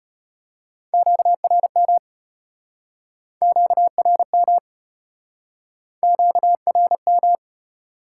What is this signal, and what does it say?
Morse code, heard as a keyed tone, about 34 wpm, 700 Hz. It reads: QRM QRM QRM